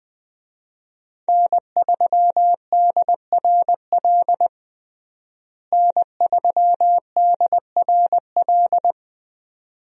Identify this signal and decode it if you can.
Morse code, heard as a keyed tone, about 20 wpm, 700 Hz. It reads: N3DRL N3DRL
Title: USA Call Signs - 5 Characters - Repeated 2x - 20wpm